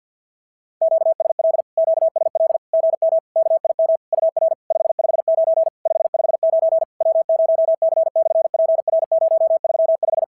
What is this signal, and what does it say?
Morse code, heard as a keyed tone, about 52 wpm, 690 Hz. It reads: QSL QSL GM KIM UR 559 559 W0C/PR035
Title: SOTA QSO Practice 50wpm